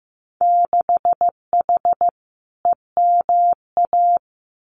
Morse code, heard as a keyed tone, about 15 wpm, 700 Hz. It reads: 6H EMA
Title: Winter Field Day Exchanges Encore 15wpm